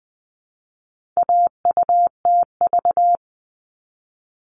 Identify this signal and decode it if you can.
Morse code, heard as a keyed tone, about 20 wpm, 700 Hz. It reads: AUTV
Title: All Cut Numbers - 4-Digit 20wpm